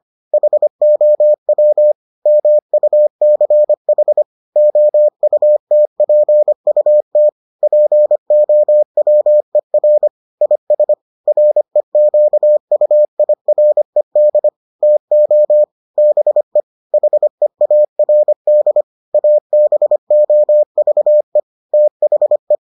Morse code, heard as a keyed tone, about 25 wpm, 600 Hz. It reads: HOW MUCH OUTPUT POWER IS REQUIRED TO BE HEARD ABOVE THE